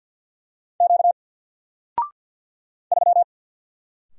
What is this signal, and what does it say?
Morse code, heard as a keyed tone, about 50 wpm, 700 Hz. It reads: Q 3
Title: Single Character (Letter:Number:Punctuation:Prosign) Alt 50wpm